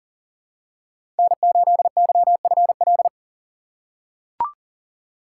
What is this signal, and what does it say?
Morse code, heard as a keyed tone, about 39 wpm, 690 Hz. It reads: N8YFL
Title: USA Call Signs - 2x Character Spacing - 40wpm